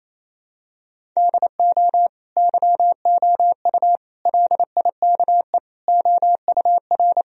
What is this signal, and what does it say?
Morse code, heard as a keyed tone, about 28 wpm, 700 Hz. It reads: DO YOU LIKE OUR
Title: Sentences from Top 1000 Words - Review Part 1 - 28wpm